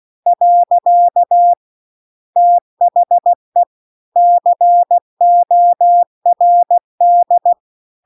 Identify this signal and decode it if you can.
Morse code, heard as a keyed tone, about 16 wpm, 700 Hz. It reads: . THE CORD
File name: CW-The-Big-Twoheaded-River-2-16WPM